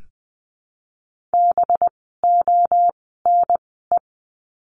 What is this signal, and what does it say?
Morse code, heard as a keyed tone, about 20 wpm, 700 Hz. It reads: B O N E
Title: 185 Lesson 30f - Letter b - Words 20wpm